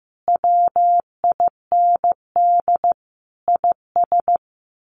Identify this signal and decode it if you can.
Morse code, heard as a keyed tone, about 15 wpm, 700 Hz. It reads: WIND IS
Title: Sentences from Top 1200 Words 15wpm